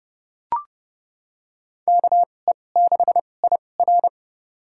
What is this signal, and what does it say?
Morse code, heard as a keyed tone, about 30 wpm, spaced as for 22 wpm, 700 Hz. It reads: KE6IR
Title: USA Call Signs - 5 Characters - 2x Character Spacing - 30wpm